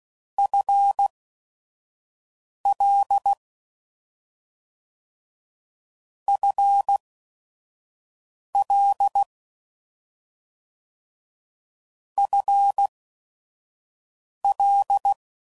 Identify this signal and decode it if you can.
Morse code, heard as a keyed tone, about 16 wpm, spaced as for 5 wpm, 800 Hz. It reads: FL FL FL